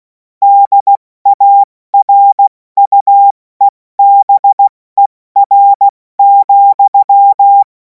Morse code, heard as a keyed tone, about 16 wpm, spaced as for 14 wpm, 800 Hz. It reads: DARUEBER,
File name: Morsekurs / Witze / Witz 24. Februar 2026 70 F80